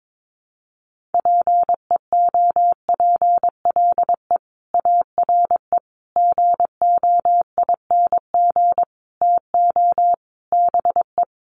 Morse code, heard as a keyed tone, about 22 wpm, 700 Hz. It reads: PEOPLE ARE GOING TO BE
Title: Sets of 5 Words Encore 22wpm